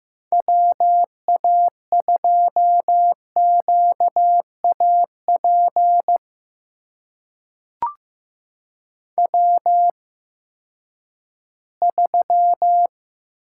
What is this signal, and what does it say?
Morse code, heard as a keyed tone, about 15 wpm, spaced as for 10 wpm, 700 Hz. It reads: WA2QAP W3
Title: USA Call Signs - 8x Character Spacing - 15wpm